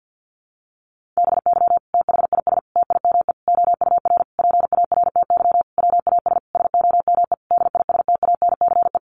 Morse code, heard as a keyed tone, about 50 wpm, 700 Hz. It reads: BY THIS TIME OUR PARTY WAS SOME DISTANCE
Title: Sentences from Top 1400 Words 50wpm